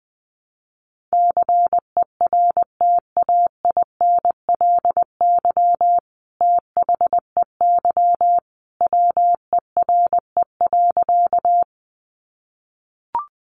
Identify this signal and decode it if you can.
Morse code, heard as a keyed tone, about 20 wpm, 700 Hz. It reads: CERTAINLY THEY WERE.
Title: Sentences from Top 900 Words 20wpm